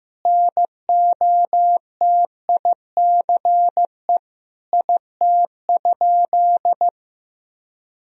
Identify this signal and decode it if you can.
Morse code, heard as a keyed tone, about 16 wpm, 700 Hz. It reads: NOTICE IT?
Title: Sentences from Top 1000 Words - Review Part 2 - 15wpm